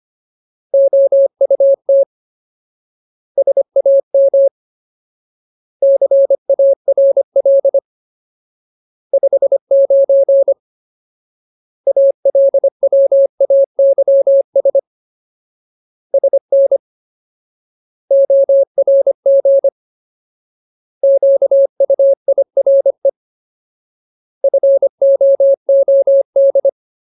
Morse code, heard as a keyed tone, about 25 wpm, 550 Hz. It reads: OUT SAM CARL 59 ALWAYS SN ORG QUIRE FOOD
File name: Ultimate Word List / 25WPM Ultimate